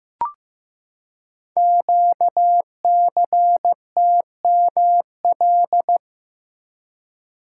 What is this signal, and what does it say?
Morse code, heard as a keyed tone, about 15 wpm, 700 Hz. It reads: QCTML
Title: Code Groups - 5 Letters - 15wpm